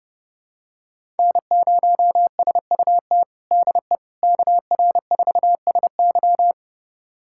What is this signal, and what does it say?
Morse code, heard as a keyed tone, about 30 wpm, 700 Hz. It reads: N0SUT DE KR4SY
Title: USA Callsign DE Callsign - 5 Characters - 30wpm